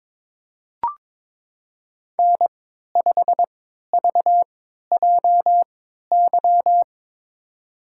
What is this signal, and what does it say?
Morse code, heard as a keyed tone, about 23 wpm, spaced as for 13 wpm, 700 Hz. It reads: N5VJY
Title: USA Call Signs - 3x Character Spacing - 22wpm